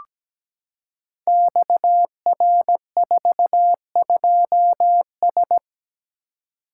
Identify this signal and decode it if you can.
Morse code, heard as a keyed tone, about 17 wpm, 700 Hz. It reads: XR42S
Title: Code Groups - 5 Letter:Numbers - 17wpm